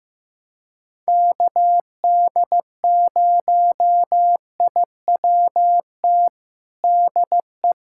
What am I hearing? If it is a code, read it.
Morse code, heard as a keyed tone, about 15 wpm, 700 Hz. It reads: KD0IWT DE
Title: USA Callsign DE Callsign - 6 Characters - 15wpm